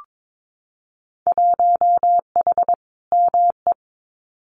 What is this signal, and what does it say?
Morse code, heard as a keyed tone, about 22 wpm, 700 Hz. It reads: 1H ME
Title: Winter Field Day Exchanges 22wpm